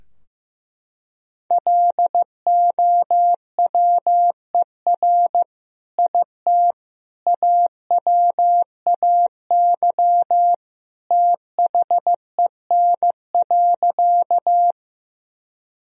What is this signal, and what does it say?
Morse code, heard as a keyed tone, about 15 wpm, 700 Hz. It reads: LOWER IT AWAY THEN.
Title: Sentences from Top 1000 Words - Review Part 4 - 15wpm